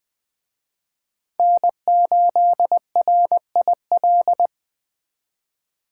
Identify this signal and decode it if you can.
Morse code, heard as a keyed tone, about 20 wpm, 700 Hz. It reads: N8RIL
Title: Speed Racing - USA 1x3 Call Signs 17wpm